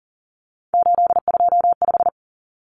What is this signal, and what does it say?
Morse code, heard as a keyed tone, about 40 wpm, 700 Hz. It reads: 825
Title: Common Cut Numbers - 3-Digit 40wpm